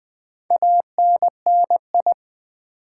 Morse code, heard as a keyed tone, about 20 wpm, 700 Hz. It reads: ANNI
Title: Build A Word - 4 Letters At A Time - 11-Letter Words - 20wpm